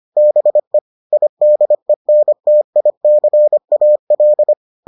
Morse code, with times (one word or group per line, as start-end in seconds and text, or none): 0.2-0.8 BE
1.1-4.5 IDENTICAL